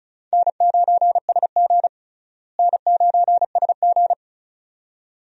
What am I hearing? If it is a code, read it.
Morse code, heard as a keyed tone, about 36 wpm, 700 Hz. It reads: N9SG N9SG
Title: USA Call Signs - 4 Characters - Repeated 2x - 35wpm